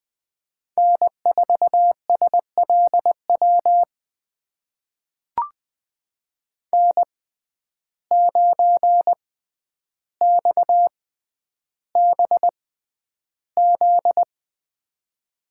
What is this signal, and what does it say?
Morse code, heard as a keyed tone, about 20 wpm, spaced as for 10 wpm, 700 Hz. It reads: N4SLW N9XBZ
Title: USA Call Signs - 6x Character Spacing - 20wpm